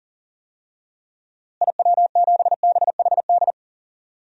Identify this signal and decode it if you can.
Morse code, heard as a keyed tone, about 40 wpm, 700 Hz. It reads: IW7BHD
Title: DXCC-100 Call Signs - 6 Characters - Repeated 3x - 40wpm